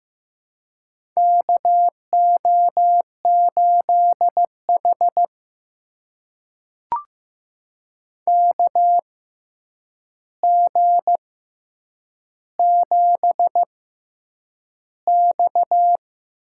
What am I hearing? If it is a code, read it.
Morse code, heard as a keyed tone, about 15 wpm, spaced as for 8 wpm, 700 Hz. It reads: KO8H KG7X
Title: USA Call Signs - 4 Characters - 6x Character Spacing - 15wpm